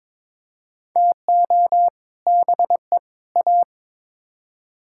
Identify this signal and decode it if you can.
Morse code, heard as a keyed tone, about 22 wpm, 700 Hz. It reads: TO BE A
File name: Build A Phrase - 5-Word Phrases - 22wpm